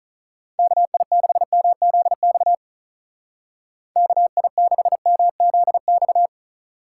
Morse code, heard as a keyed tone, about 44 wpm, 700 Hz. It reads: KI6MZX KI6MZX